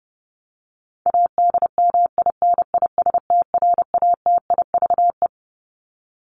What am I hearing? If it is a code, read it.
Morse code, heard as a keyed tone, about 30 wpm, 700 Hz. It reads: ADMINISTRATIVE
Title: Top 1501-1750 Words 30wpm